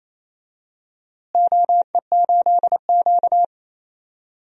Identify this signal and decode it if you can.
Morse code, heard as a keyed tone, about 28 wpm, 700 Hz. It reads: OE8Q
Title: DXCC-100 Call Signs - 4 Characters - Repeated 3x - 28wpm